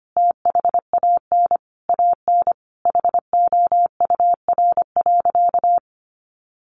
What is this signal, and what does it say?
Morse code, heard as a keyed tone, about 25 wpm, 700 Hz. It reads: THAN AN HOUR.
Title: Sentences from Top 500 Words 25wpm